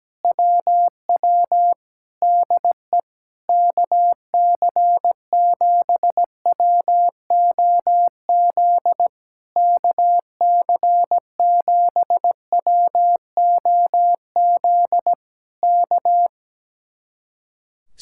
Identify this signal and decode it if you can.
Morse code, heard as a keyed tone, about 17 wpm, 700 Hz. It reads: WW DE KC7WOZ KC7WOZ K
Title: Speed-Racing Common QSO Phrases - 17wpm